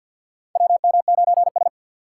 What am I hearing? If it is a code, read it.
Morse code, heard as a keyed tone, about 47 wpm, 700 Hz. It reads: WM9S
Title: USA Call Signs - 4 Characters - 3x Character Spacing - 50wpm